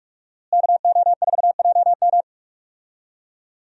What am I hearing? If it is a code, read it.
Morse code, heard as a keyed tone, about 46 wpm, 700 Hz. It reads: KO4JM
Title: USA Call Signs - 5 Characters - Repeated 2x - 45wpm